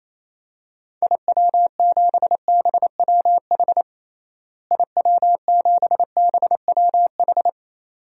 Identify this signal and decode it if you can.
Morse code, heard as a keyed tone, about 28 wpm, 700 Hz. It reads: IW7BWH IW7BWH
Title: DXCC-100 Call Signs - 6 Characters - Repeated 2x - 28wpm